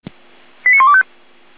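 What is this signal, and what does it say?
Five-tone selective call (ZVEI), heard as quick runs of five tones, tones of about 70 ms. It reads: 89125
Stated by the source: ZVEI five-tone selective call received with an SDR.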